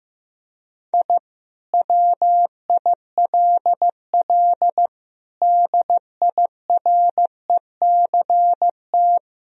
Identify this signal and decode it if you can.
Morse code, heard as a keyed tone, about 16 wpm, 700 Hz. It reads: I WILL DIRECT
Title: Sentences from Top 1000 Words - Review Part 2 - 15wpm